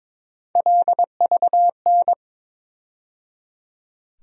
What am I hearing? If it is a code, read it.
Morse code, heard as a keyed tone, about 21 wpm, 700 Hz. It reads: LVN
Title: USA Call Sign Postfixes - 3 Letters 22wpm